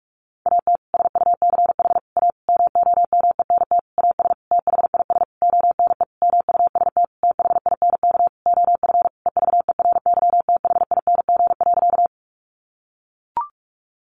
Morse code, heard as a keyed tone, about 44 wpm, 700 Hz. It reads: AT SUCH A MOMENT AS THIS ONE MUST THINK OF EVERYTHING.
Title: Sentences from Top 1000 Words 45wpm